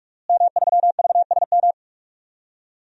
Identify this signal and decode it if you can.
Morse code, heard as a keyed tone, about 46 wpm, 700 Hz. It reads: M3VSM
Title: DXCC-100 Call Signs - 5 Characters - Repeated 3x - 45wpm